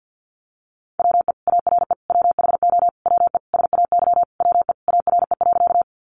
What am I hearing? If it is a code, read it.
Morse code, heard as a keyed tone, about 50 wpm, 700 Hz. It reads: WE ARE WHO WE SAY WE ARE.
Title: Sentences from Top 200 Words 50wpm